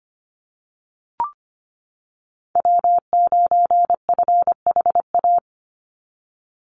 Morse code, heard as a keyed tone, about 25 wpm, 700 Hz. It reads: W9FHA
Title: NA SOTA Chaser Callsigns 25wpm